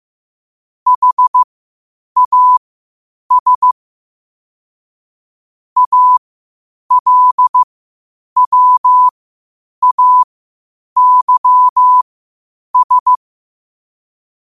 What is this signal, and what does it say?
Morse code, heard as a keyed tone, about 15 wpm, spaced as for 8 wpm, 1000 Hz. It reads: HAS ALWAYS
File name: qotdfiles / 8wpm